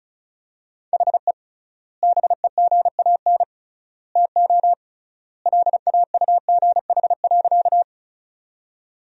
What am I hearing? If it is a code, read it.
Morse code, heard as a keyed tone, about 36 wpm, 700 Hz. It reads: HE BEGAN TO LAUGH.
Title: 3x Word-Spacing - Sentences from Top 500 Words - 35wpm